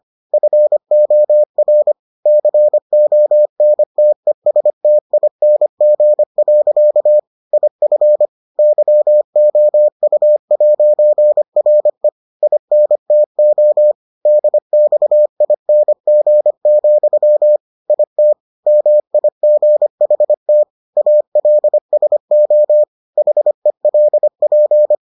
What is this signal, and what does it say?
Morse code, heard as a keyed tone, about 25 wpm, 600 Hz. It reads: FOR CONTESTING. IF YOU'RE INTO DXING, IT MIGHT ALSO HELP